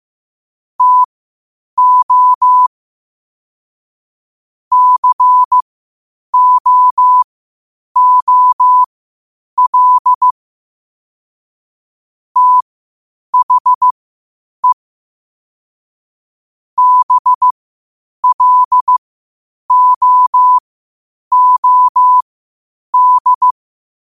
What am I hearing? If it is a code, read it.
Morse code, heard as a keyed tone, about 15 wpm, spaced as for 8 wpm, 1000 Hz. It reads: TO COOL THE BLOOD
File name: qotdfiles / 8wpm